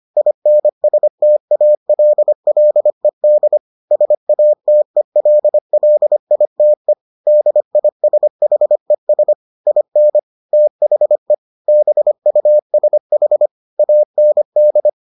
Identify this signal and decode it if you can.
Morse code, heard as a keyed tone, about 25 wpm, 600 Hz. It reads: INSTALLED SATELLITE DISHES IN THE BUSH AND